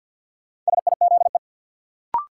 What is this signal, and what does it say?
Morse code, heard as a keyed tone, about 52 wpm, 700 Hz. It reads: SIZE
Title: Sentences from Top 400 Words - New Word Drill - 50wpm